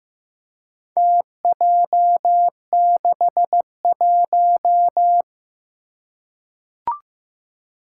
Morse code, heard as a keyed tone, about 15 wpm, 700 Hz. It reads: TJ61
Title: Four Challenging Letter-Number Combinations 15wpm